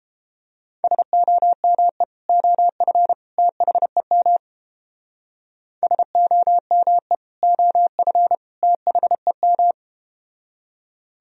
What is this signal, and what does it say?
Morse code, heard as a keyed tone, about 31 wpm, 700 Hz. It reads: SOME OF THEM SOME OF THEM